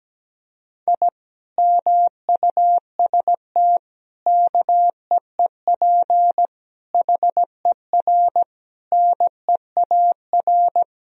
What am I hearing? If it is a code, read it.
Morse code, heard as a keyed tone, about 17 wpm, 700 Hz. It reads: I MUST KEEP HER NEAR
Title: Sentences from Top 1000 Words - Review Part 1 - 17wpm